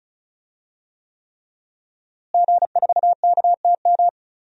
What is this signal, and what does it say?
Morse code, heard as a keyed tone, about 34 wpm, 700 Hz. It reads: G4KTM
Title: DXCC-100 Call Signs - 5 Characters - Repeated 3x - 35wpm